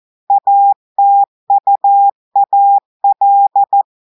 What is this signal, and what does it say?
Morse code, heard as a keyed tone, about 14 wpm, 800 Hz. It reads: ATUAL
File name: audios / portugais / portugais texte en clair 1 vitesse 14 mots minutes